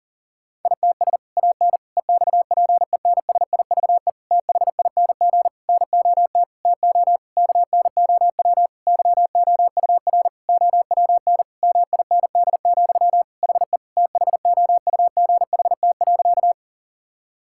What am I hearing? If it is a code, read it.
Morse code, heard as a keyed tone, about 37 wpm, 700 Hz. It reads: ITS AN EXPENSIVE THING NOT TO KNOW YOUR OWN MIND, HE THOUGHT.
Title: Sentences from Top 1000 Words - Review Part 6 - 40wpm